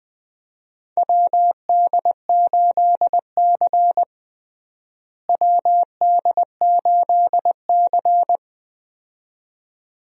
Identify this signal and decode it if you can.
Morse code, heard as a keyed tone, about 20 wpm, 700 Hz. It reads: WD8C WD8C
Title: USA Call Signs - 4 Characters - Repeated 2x - 20wpm